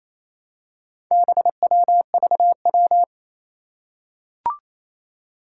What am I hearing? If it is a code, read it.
Morse code, heard as a keyed tone, about 28 wpm, 700 Hz. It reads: BWVW
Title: Four Challenging Letter-Number Combinations 28wpm